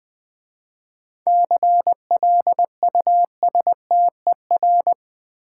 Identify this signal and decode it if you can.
Morse code, heard as a keyed tone, about 20 wpm, 700 Hz. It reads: CLUSTER
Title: 7 Letter Words - First 3 Letters Given - 20wpm